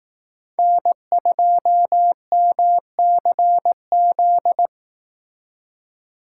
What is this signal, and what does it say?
Morse code, heard as a keyed tone, about 18 wpm, 700 Hz. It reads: N2MCZ
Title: Speed Racing - USA 1x3 Call Signs 15wpm